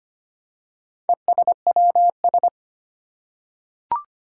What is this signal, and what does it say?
Morse code, heard as a keyed tone, about 25 wpm, 700 Hz. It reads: ESWS
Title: Four Challenging Letter-Number Combinations 25wpm